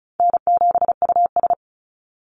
Morse code, heard as a keyed tone, about 34 wpm, 700 Hz. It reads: N7US
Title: Call Signs Repeated 2x - 35wpm